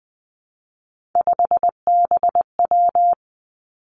Morse code, heard as a keyed tone, about 20 wpm, 700 Hz. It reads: 5BW
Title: Three Challenging Letter-Number Combinations 20wpm